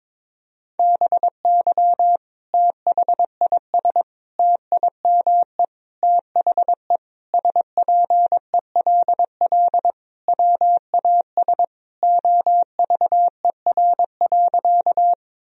Morse code, heard as a keyed tone, about 22 wpm, 700 Hz. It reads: BY THIS TIME THE SPELL WAS OVER.
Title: Sentences from Top 1000 Words - Review Part 1 - 22wpm